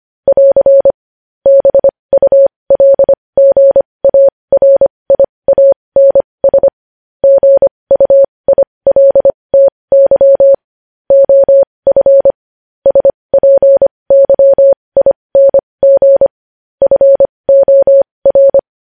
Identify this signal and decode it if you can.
Morse code, heard as a keyed tone, about 25 wpm, spaced as for 20 wpm, 550 Hz. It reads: <AR> BULGARIANS GUILTY OF SPYING FOR
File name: podcast / News headlines-1x-25x20